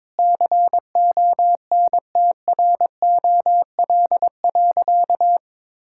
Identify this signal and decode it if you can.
Morse code, heard as a keyed tone, about 22 wpm, 700 Hz. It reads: CONTROL.
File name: Sentences with 9-Word Phrases - 22wpm